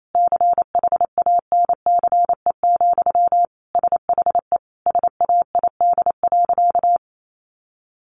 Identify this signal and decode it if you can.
Morse code, heard as a keyed tone, about 28 wpm, 700 Hz. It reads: CHANCE, SHE SAID.
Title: Sentences from Top 800 Words 28wpm